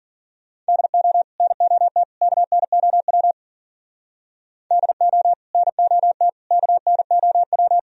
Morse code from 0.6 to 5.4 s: DO NOT KNOW DO

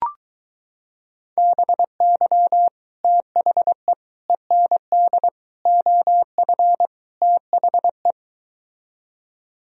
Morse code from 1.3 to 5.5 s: BY THE END

